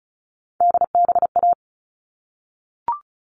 Morse code, heard as a keyed tone, about 34 wpm, 700 Hz. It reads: DBA
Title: All Cut Numbers - 3-Digit 35wpm